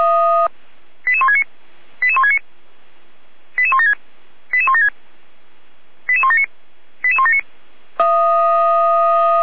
Five-tone selective call (ZVEI), heard as quick runs of five tones, tones of about 70 ms. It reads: 88169 88169 88176 88176 88179 88179